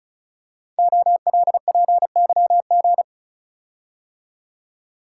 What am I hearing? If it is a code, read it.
Morse code, heard as a keyed tone, about 35 wpm, 690 Hz. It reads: OLPYG
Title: Code Groups - 5 Letters - 35wpm